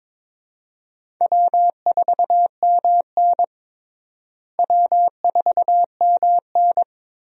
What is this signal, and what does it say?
Morse code, heard as a keyed tone, about 22 wpm, 700 Hz. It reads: W4MN W4MN
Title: USA Call Signs - 4 Characters - Repeated 3x - 22wpm